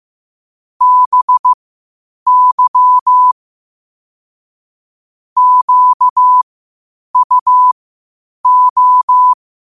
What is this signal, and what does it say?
Morse code, heard as a keyed tone, about 15 wpm, spaced as for 8 wpm, 1000 Hz. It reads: BY QUO
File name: qotdfiles / 8wpm